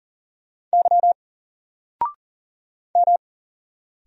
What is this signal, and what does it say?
Morse code, heard as a keyed tone, about 40 wpm, 700 Hz. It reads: Y M
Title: Single Letter-Number 40wpm